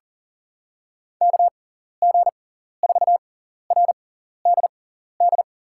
Morse code, heard as a keyed tone, about 40 wpm, spaced as for 14 wpm, 700 Hz. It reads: KG4RDD
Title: USA Call Signs - 6 Characters - 6x Character Spacing - 40wpm